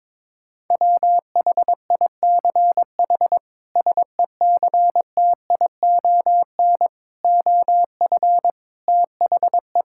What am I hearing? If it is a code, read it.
Morse code, heard as a keyed tone, about 22 wpm, 700 Hz. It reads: WHICH SECTION OF THE
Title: Sentences from Top 1000 Words - Review Part 5 - 22wpm